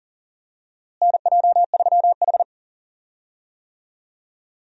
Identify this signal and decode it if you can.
Morse code, heard as a keyed tone, about 40 wpm, 700 Hz. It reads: NJ3H
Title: USA Call Signs - 4 Characters - Repeated 3x - 40wpm